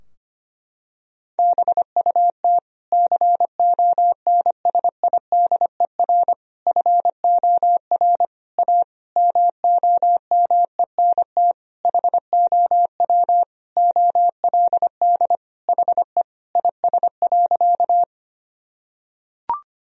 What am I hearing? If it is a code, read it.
Morse code, heard as a keyed tone, about 25 wpm, 700 Hz. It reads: BUT CONSIDER FOR A MOMENT HOW OLD HE IS.
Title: Sentences from Top 1000 Words - Review Part 4 - 25wpm